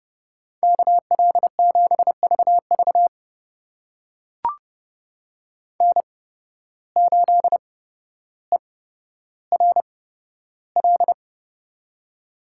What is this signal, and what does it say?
Morse code, heard as a keyed tone, about 29 wpm, spaced as for 13 wpm, 700 Hz. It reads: KL7VV N8ERL
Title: USA Call Signs - 5 Characters - 8x Character Spacing - 30wpm